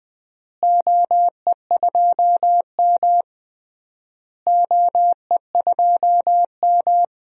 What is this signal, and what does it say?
Morse code, heard as a keyed tone, about 20 wpm, 700 Hz. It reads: OE2M OE2M
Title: DXCC-100 Call Signs - 4 Characters - Repeated 3x - 20wpm